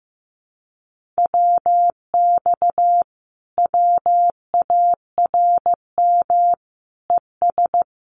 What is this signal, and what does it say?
Morse code, heard as a keyed tone, about 15 wpm, 700 Hz. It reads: WX WARM ES
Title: Common QSO Phrases Encore 15wpm